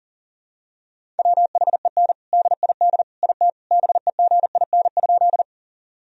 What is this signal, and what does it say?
Morse code, heard as a keyed tone, about 37 wpm, 700 Hz. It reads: WHEN DID IT BEGIN?
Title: Sentences from Top 1000 Words - Review Part 1 - 40wpm